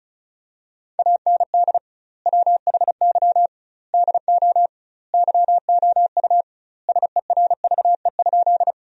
Morse code, heard as a keyed tone, about 36 wpm, 700 Hz. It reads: AND WHY DO YOU SERVE?
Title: 2x Word-Spacing - Sentences from Top 400 Words - 35wpm